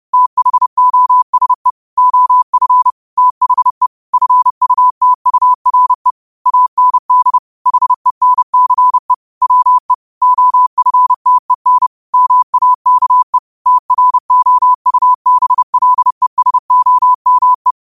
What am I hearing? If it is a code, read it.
Morse code, heard as a keyed tone, about 30 wpm, 1000 Hz. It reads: THOSE OF THE FUTURE AND HENCE WE OFTEN MAKE TROUBLESOME
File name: qotdfiles / 30wpm